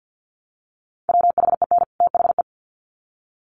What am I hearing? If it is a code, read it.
Morse code, heard as a keyed tone, about 50 wpm, 700 Hz. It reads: WHEN THE
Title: Sets of 2 Words 50wpm